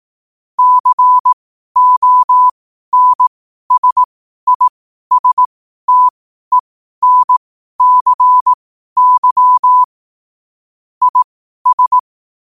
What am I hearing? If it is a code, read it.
Morse code, heard as a keyed tone, about 18 wpm, spaced as for 12 wpm, 1000 Hz. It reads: CONSISTENCY IS
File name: qotdfiles / 12wpm